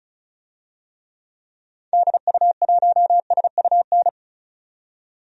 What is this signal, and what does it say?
Morse code, heard as a keyed tone, about 36 wpm, 700 Hz. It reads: DU1SUN
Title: DXCC-100 Call Signs - 6 Characters - 35wpm